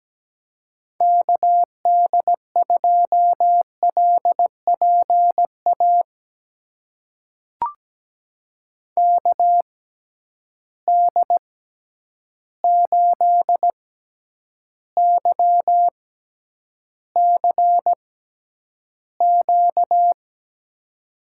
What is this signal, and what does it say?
Morse code, heard as a keyed tone, about 17 wpm, spaced as for 9 wpm, 700 Hz. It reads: KD2LPA KD8YCQ
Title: USA Call Signs - 6 Characters - 6x Character Spacing - 17wpm